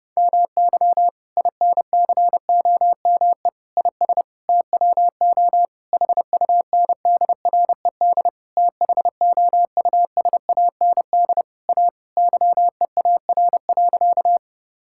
Morse code, heard as a keyed tone, about 32 wpm, 700 Hz. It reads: MY INCOME IS TWO HUNDRED THOUSAND A YEAR.
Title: Sentences from Top 1000 Words - Review Part 5 - 30wpm